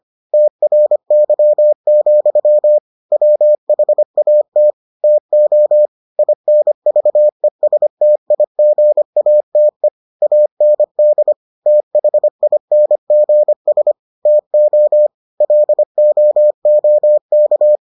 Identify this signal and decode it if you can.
Morse code, heard as a keyed tone, about 25 wpm, 600 Hz. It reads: TRY, WHAT TO INVESTIGATE AND THINGS TO LOOK